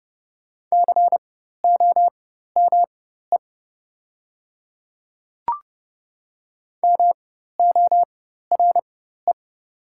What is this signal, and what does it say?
Morse code, heard as a keyed tone, about 30 wpm, spaced as for 14 wpm, 700 Hz. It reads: COME MORE
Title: 4x Character Spacing - Top 100 Words - 30wpm